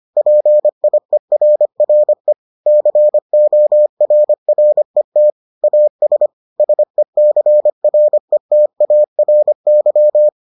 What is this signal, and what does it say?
Morse code, heard as a keyed tone, about 25 wpm, 600 Hz. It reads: PIERRE CORRET AS SECRETARY